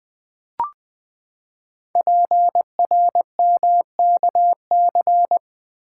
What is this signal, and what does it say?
Morse code, heard as a keyed tone, about 20 wpm, 700 Hz. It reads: PRMKC
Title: Code Groups - 5 Letters - 20wpm